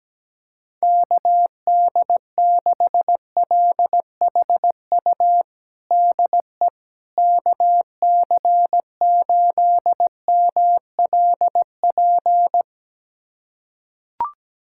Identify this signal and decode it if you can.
Morse code, heard as a keyed tone, about 17 wpm, 700 Hz. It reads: KD6LHU DE KC8MLP
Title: USA Callsign DE Callsign 17wpm